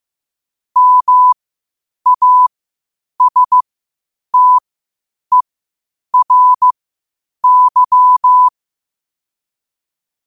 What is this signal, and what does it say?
Morse code, heard as a keyed tone, about 15 wpm, spaced as for 8 wpm, 1000 Hz. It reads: MASTERY
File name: qotdfiles / 8wpm